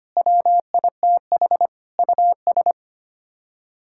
Morse code, heard as a keyed tone, about 25 wpm, 700 Hz. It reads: WITH US
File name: Sets of 9 Words - 25wpm